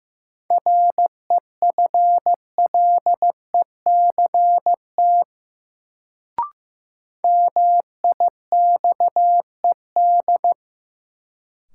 Morse code, heard as a keyed tone, about 16 wpm, 700 Hz. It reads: REFLECT MIXED
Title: Sentences from Top 1500 Words - New Word Drill - 15wpm